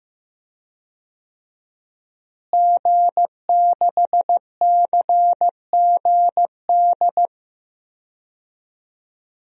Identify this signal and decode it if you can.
Morse code, heard as a keyed tone, about 15 wpm, 700 Hz. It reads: G6CGD
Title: DXCC-100 Call Signs - 5 Characters - Repeated 2x - 15wpm